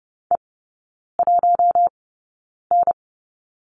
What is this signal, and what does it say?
Morse code, heard as a keyed tone, about 30 wpm, spaced as for 9 wpm, 700 Hz. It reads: E1N
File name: Lesson 18e - Letter m - Three Characters 30wpm